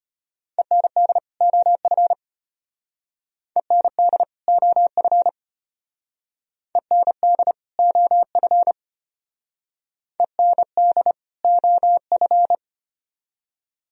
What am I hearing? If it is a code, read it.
Morse code, heard as a keyed tone, about 32 wpm, 700 Hz. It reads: END OF END OF END OF END OF